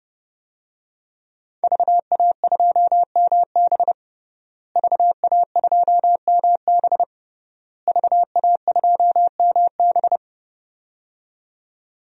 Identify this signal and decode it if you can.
Morse code, heard as a keyed tone, about 30 wpm, 700 Hz. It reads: VA2MB VA2MB VA2MB
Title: DXCC-100 Call Signs - 5 Characters - Repeated 3x - 30wpm